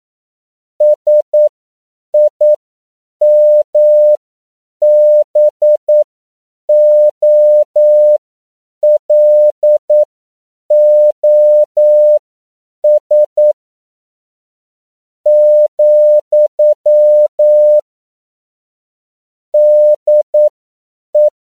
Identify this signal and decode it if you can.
Morse code, heard as a keyed tone, about 9 wpm, spaced as for 7 wpm, 600 Hz. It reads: SIMBOLOS , DE
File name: CW / 10PPM